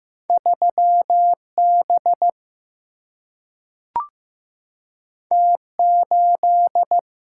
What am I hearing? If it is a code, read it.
Morse code, heard as a keyed tone, about 15 wpm, 700 Hz. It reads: 3 B T 8
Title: Single-Letter Single-Number 15wpm